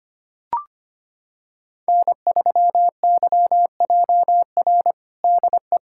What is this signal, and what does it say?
Morse code, heard as a keyed tone, about 25 wpm, 700 Hz. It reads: N3YJR DE
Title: USA Callsign DE Callsign - 5 Characters - 25wpm